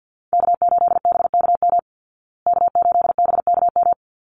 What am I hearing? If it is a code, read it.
Morse code, heard as a keyed tone, about 52 wpm, 700 Hz. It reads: K8BKM K8BKM
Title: Call Signs Repeated 3x - 50wpm